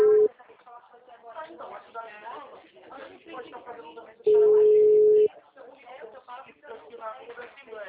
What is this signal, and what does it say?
Ringback tone (Europe, 425 Hz), with speech.